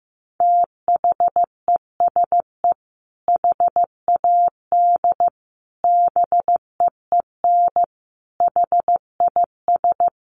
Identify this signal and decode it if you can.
Morse code, heard as a keyed tone, about 15 wpm, 700 Hz. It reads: THESE HAD BEEN HIS
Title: Sentences from Top 1500 Words 15wpm